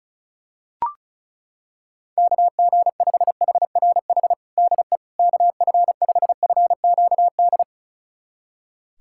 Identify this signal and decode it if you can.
Morse code, heard as a keyed tone, about 34 wpm, 700 Hz. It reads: KG5HRH DE KF5FQD
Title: USA Callsign DE Callsign - 6 Characters - 35wpm